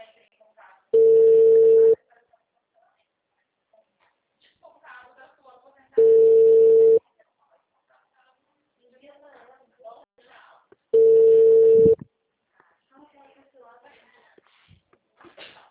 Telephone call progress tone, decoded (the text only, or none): ringback tone